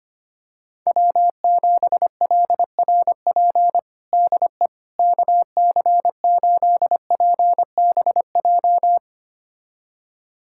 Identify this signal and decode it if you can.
Morse code, heard as a keyed tone, about 25 wpm, 700 Hz. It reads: W7LRP DE KC8PBJ
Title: USA Callsign DE Callsign 25wpm